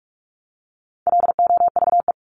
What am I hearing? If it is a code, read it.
Morse code, heard as a keyed tone, about 44 wpm, 700 Hz. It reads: LOVE
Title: Top 500 Words Repeated 45wpm